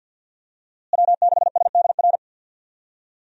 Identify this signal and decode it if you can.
Morse code, heard as a keyed tone, about 52 wpm, 700 Hz. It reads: W6SDR
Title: USA Call Signs - 5 Characters - Repeated 2x - 50wpm